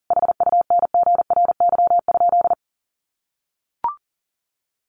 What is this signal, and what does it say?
Morse code, heard as a keyed tone, about 40 wpm, 700 Hz. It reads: HUNGRY?
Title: Sentences from Top 1400 Words 40wpm